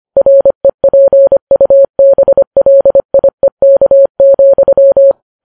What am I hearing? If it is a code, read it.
Morse code, heard as a keyed tone, about 25 wpm, 550 Hz. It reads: REPUBLIEK,